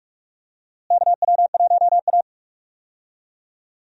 Morse code, heard as a keyed tone, about 46 wpm, 700 Hz. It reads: KW1A
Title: USA Call Signs - 4 Characters - Repeated 2x - 45wpm